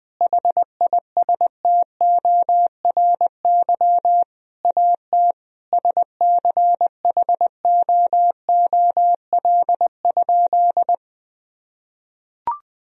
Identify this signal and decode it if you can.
Morse code, heard as a keyed tone, about 20 wpm, 700 Hz. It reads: HISTORY AT SCHOOL?
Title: Sentences from Top 1000 Words - Review Part 4 - 20wpm